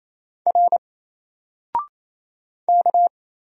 Morse code, heard as a keyed tone, about 27 wpm, 700 Hz. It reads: R K
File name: Single-Letter-Number-28wpm